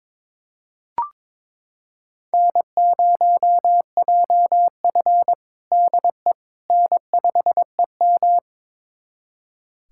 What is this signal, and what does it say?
Morse code, heard as a keyed tone, about 22 wpm, 700 Hz. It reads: N0JF DE N5EM
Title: USA Callsign DE Callsign - 4 Characters - 22wpm